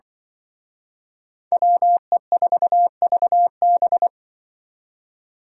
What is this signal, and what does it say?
Morse code, heard as a keyed tone, about 25 wpm, 700 Hz. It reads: WE4VB
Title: Speed Racing - USA 2x2 Call Signs 20wpm